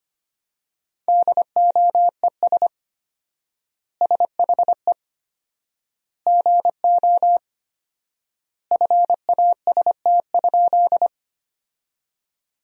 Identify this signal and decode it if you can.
Morse code, heard as a keyed tone, about 25 wpm, 700 Hz. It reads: DOES SHE GO FAST?
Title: 4x Word-Spacing - Sentences from Top 400 Words - 25wpm